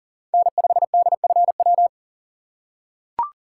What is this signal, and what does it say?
Morse code, heard as a keyed tone, about 37 wpm, 700 Hz. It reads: N5DFW
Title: USA Call Signs - 5 Characters - 4x Character Spacing - 40wpm